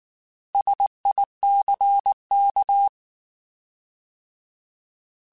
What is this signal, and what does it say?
Morse code, heard as a keyed tone, about 19 wpm, 800 Hz. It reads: SICK
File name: Alphabetical / Alphabetical mp3s / Ss @18